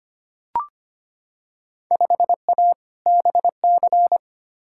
Morse code, heard as a keyed tone, about 25 wpm, 700 Hz. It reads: 5A BC
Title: Field Day Exchanges - 25wpm